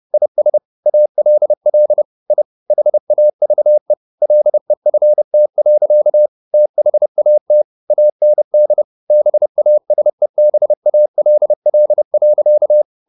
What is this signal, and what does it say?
Morse code, heard as a keyed tone, about 30 wpm, 600 Hz. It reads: IS ALL I HAVE LEFT. THAT AND BASEBALL.